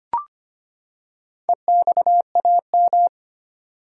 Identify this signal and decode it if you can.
Morse code, heard as a keyed tone, about 25 wpm, 700 Hz. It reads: EXAM
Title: Four Letter Words - 25wpm